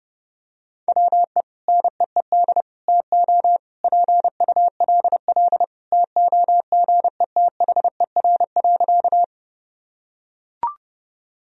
Morse code, heard as a keyed tone, about 30 wpm, 700 Hz. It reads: WE NEED TO PULL TOGETHER.
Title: Sentences from Top 1000 Words - Review Part 2 - 30wpm